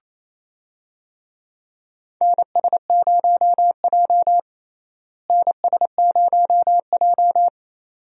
Morse code, heard as a keyed tone, about 28 wpm, 700 Hz. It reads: NS0J NS0J
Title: USA Call Signs - 4 Characters - Repeated 2x - 28wpm